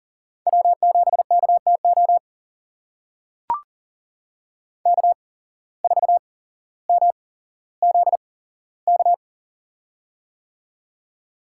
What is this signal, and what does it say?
Morse code, heard as a keyed tone, about 38 wpm, spaced as for 17 wpm, 700 Hz. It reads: W7KTO K4MZK
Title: USA Call Signs - 5 Characters - 8x Character Spacing - 40wpm